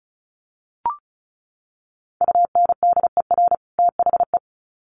Morse code, heard as a keyed tone, about 34 wpm, 700 Hz. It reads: UNDER THE
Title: Sets of 2 Words 35wpm